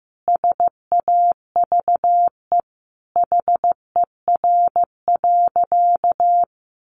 Morse code, heard as a keyed tone, about 15 wpm, 700 Hz. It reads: SAVE HER.
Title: Sentences from Top 1100 Words 15wpm